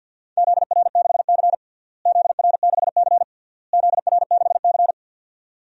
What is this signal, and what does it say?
Morse code, heard as a keyed tone, about 48 wpm, 700 Hz. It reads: ZR6C ZR6C ZR6C
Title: DXCC-100 Call Signs - 4 Characters - Repeated 3x - 50wpm